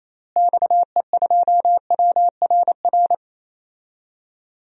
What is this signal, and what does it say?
Morse code, heard as a keyed tone, about 28 wpm, 700 Hz. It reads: XE2WRR
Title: DXCC-100 Call Signs - 6 Characters - Repeated 2x - 28wpm